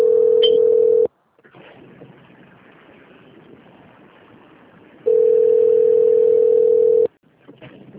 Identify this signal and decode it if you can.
Ringback tone (North America), 440 + 480 Hz.